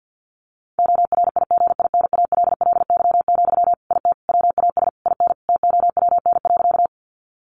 Morse code, heard as a keyed tone, about 50 wpm, 700 Hz. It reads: ORIGINALLY, IT WAS IN TOWN.
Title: Sentences from Top 1500 Words 50wpm